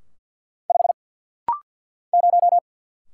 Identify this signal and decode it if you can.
Morse code, heard as a keyed tone, about 50 wpm, 700 Hz. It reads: 5 0
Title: Single Letter-Number - Rapid-Fire 50wpm